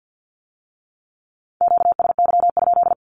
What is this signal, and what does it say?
Morse code, heard as a keyed tone, about 50 wpm, 700 Hz. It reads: QSY?
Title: Q-Codes with Spoken Meaning - 50wpm